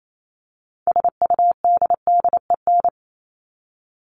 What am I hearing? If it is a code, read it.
Morse code, heard as a keyed tone, about 28 wpm, 700 Hz. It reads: SUDDEN
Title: LY Suffix - Top 100 Words with Base Words 28wpm